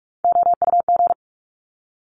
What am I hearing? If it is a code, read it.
Morse code, heard as a keyed tone, about 46 wpm, 700 Hz. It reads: OUG
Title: Top Three Letter Combinations in Words 45wpm